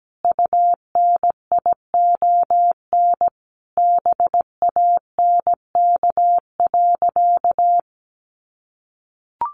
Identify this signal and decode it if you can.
Morse code, heard as a keyed tone, about 17 wpm, 700 Hz. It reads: UNION BANK.
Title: Sentences from Top 1400 Words 17wpm